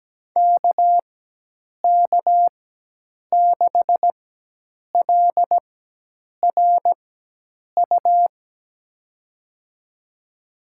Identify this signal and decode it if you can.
Morse code, heard as a keyed tone, about 17 wpm, spaced as for 8 wpm, 700 Hz. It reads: KK6LRU
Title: USA Call Signs - 6 Characters - 4x Character Spacing - 17wpm